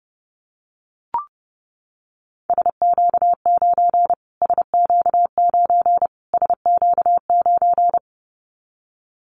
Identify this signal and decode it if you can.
Morse code, heard as a keyed tone, about 30 wpm, 690 Hz. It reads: SQ9 SQ9 SQ9
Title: Callsign Prefixes for Top 25 DXCC - Repeated 3x - 30wpm